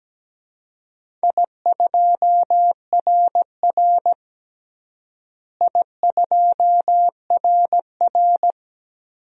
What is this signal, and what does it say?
Morse code, heard as a keyed tone, about 17 wpm, 700 Hz. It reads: I2RR I2RR
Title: DXCC-100 Call Signs - 4 Characters - Repeated 3x - 17wpm